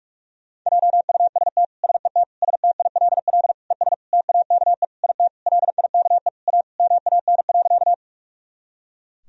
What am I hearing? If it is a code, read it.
Morse code, heard as a keyed tone, about 46 wpm, 700 Hz. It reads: JUST SET STILL ES TAKE IT LIKE A MAN.
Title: Sentences from Top 300 Words and Abbreviations 45wpm